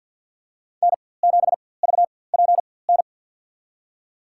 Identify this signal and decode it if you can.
Morse code, heard as a keyed tone, about 50 wpm, spaced as for 23 wpm, 700 Hz. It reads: N7VPN
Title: USA Call Signs - 5 Characters - 4x Character Spacing - 50wpm